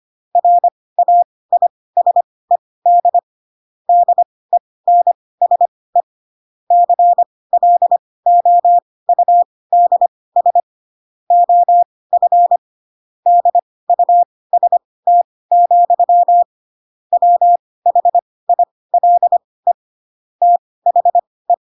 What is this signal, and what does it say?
Morse code, heard as a keyed tone, about 25 wpm, spaced as for 18 wpm, 700 Hz. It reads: RAISED DENSE CLOUDS OF DUST, WHILE THE
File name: Pinocchio / Pinocchio ch6 12 25 ik1hge